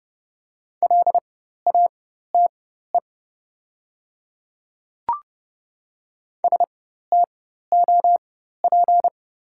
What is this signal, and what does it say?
Morse code, heard as a keyed tone, about 30 wpm, spaced as for 14 wpm, 700 Hz. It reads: LATE STOP
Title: 4x Character Spacing - Top 300 Words - 30wpm